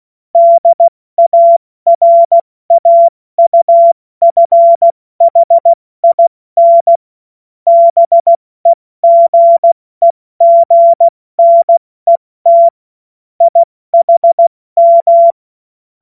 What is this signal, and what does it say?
Morse code, heard as a keyed tone, about 16 wpm, 680 Hz. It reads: DARAUFHIN BEGEGNET IHM
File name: Witz-26-August-2025-70-F80